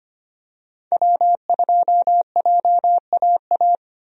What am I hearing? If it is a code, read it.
Morse code, heard as a keyed tone, about 25 wpm, 700 Hz. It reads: W2JAA
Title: USA Call Signs - 5 Characters - 25wpm